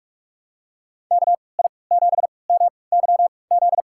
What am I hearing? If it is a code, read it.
Morse code, heard as a keyed tone, about 46 wpm, spaced as for 26 wpm, 700 Hz. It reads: KI7MYZ
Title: USA Call Signs - 3x Character Spacing - 45wpm